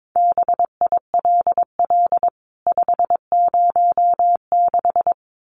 Morse code, heard as a keyed tone, about 22 wpm, 700 Hz. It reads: BILL 506
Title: CWT Exchange 22wpm